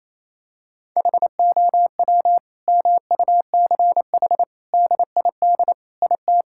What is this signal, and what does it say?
Morse code, heard as a keyed tone, about 28 wpm, 700 Hz. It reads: HOW MUCH DID IT
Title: Sentences from Top 1000 Words - Review Part 5 - 28wpm